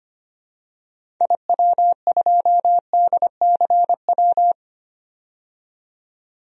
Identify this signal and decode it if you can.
Morse code, heard as a keyed tone, about 25 wpm, 700 Hz. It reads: IW2DCW
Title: DXCC-100 Call Signs - 6 Characters - 25wpm